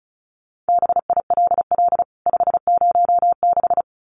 Morse code, heard as a keyed tone, about 34 wpm, 700 Hz. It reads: BILL 506
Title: CWT Exchange 35wpm